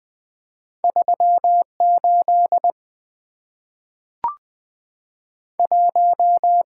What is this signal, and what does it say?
Morse code, heard as a keyed tone, about 20 wpm, 700 Hz. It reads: 38 1
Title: Two Letter-Number Combinations - 20wpm